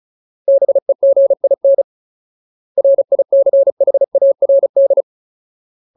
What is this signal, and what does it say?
Morse code, heard as a keyed tone, about 34 wpm, 550 Hz. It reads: BEGIN RICHARD
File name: Ultimate Word List / 35WPM Ultimate